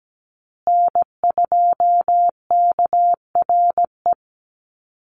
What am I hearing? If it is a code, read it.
Morse code, heard as a keyed tone, about 17 wpm, 700 Hz. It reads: N2KRE
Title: USA Call Signs Repeated 2x 17wpm